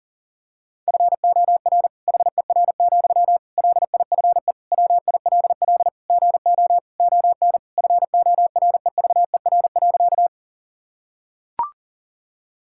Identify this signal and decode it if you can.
Morse code, heard as a keyed tone, about 40 wpm, 700 Hz. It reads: FOR HER, LIFE WILL GO ON FOREVER.
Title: Sentences from Top 1600 Words 40wpm